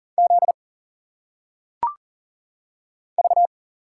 Morse code, heard as a keyed tone, about 40 wpm, 700 Hz. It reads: Z V
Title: Single Letters - Phonetic Pronunciation 40wpm